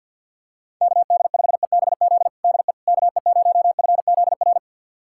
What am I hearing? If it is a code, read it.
Morse code, heard as a keyed tone, about 50 wpm, 700 Hz. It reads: KD5EBZ DE KE0FZR
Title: USA Callsign DE Callsign - 6 Characters - 50wpm